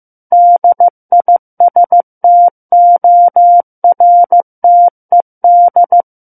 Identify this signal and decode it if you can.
Morse code, heard as a keyed tone, about 15 wpm, 700 Hz. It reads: DISTORTED
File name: Book One / 06-Chapter 6 - 15 WPM